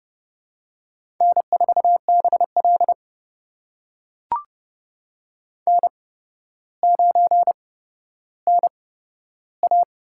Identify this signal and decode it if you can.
Morse code, heard as a keyed tone, about 28 wpm, spaced as for 13 wpm, 700 Hz. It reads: N4BL N9NA
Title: USA Call Signs - 4 Characters - 8x Character Spacing - 30wpm